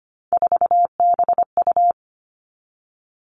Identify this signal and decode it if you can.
Morse code, heard as a keyed tone, about 25 wpm, 700 Hz. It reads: 4BU
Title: Three Challenging Letter-Number Combinations 25wpm